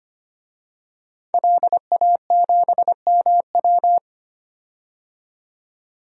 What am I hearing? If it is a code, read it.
Morse code, heard as a keyed tone, about 25 wpm, 700 Hz. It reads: LA7MW
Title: DXCC-100 Call Signs - 5 Characters - 25wpm